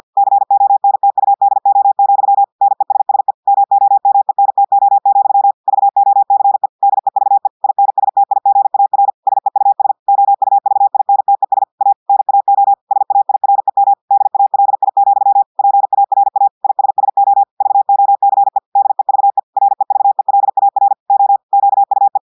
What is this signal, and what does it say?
Morse code, heard as a keyed tone, about 50 wpm, 800 Hz. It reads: CONTUDO, NESSE MOMENTO, VOCE DEVE INSTIGAR SEUS OUVINTES A NAO SAIREM DALI, PARA ISSO VOCE DEVE REVELAR O QUE
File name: portugais texte en clair 2 vitesse 50 mots minutes